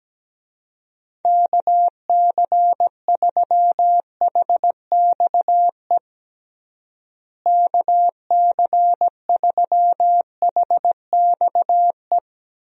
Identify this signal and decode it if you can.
Morse code, heard as a keyed tone, about 17 wpm, 700 Hz. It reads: KC3HXE KC3HXE
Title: USA Call Signs - 6 Characters - Repeated 3x - 17wpm